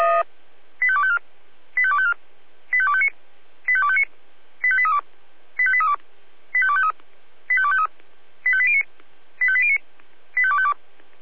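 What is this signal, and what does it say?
Five-tone selective call (ZVEI), heard as quick runs of five tones, tones of about 70 ms. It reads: 85254 85254 85259 85259 85832 85832 85253 85253 85809 85809 85252
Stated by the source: ZVEI five-tone selective call received on an SDR.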